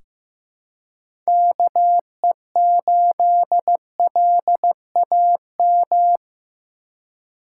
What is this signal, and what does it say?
Morse code, heard as a keyed tone, about 15 wpm, 700 Hz. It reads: KE8LAM
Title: USA Call Signs - 3x Character Spacing - 15wpm